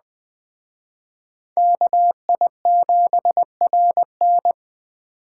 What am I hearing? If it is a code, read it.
Morse code, heard as a keyed tone, about 21 wpm, 700 Hz. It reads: KI7RN